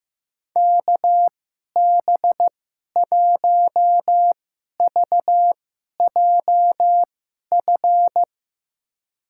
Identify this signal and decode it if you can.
Morse code, heard as a keyed tone, about 15 wpm, spaced as for 11 wpm, 700 Hz. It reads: KB1VJF
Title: USA Call Signs - 6 Characters - 2x Character Spacing - 15wpm